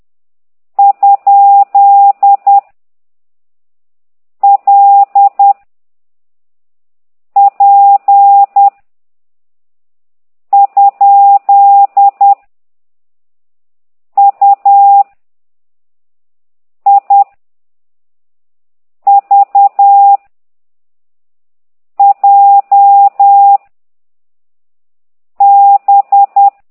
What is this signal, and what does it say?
Morse code, heard as a keyed tone, about 10 wpm, 800 Hz. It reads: ? L P ? U I V J B
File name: ct2idw / exerc9